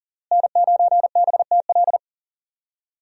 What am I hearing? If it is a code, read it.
Morse code, heard as a keyed tone, about 40 wpm, 700 Hz. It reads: N9BTL
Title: USA Call Signs - 5 Characters - Repeated 3x - 40wpm